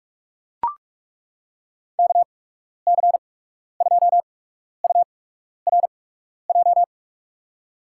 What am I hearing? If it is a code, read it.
Morse code, heard as a keyed tone, about 42 wpm, spaced as for 12 wpm, 700 Hz. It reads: KC2URJ
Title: USA Call Signs - 6 Characters - 8x Character Spacing - 45wpm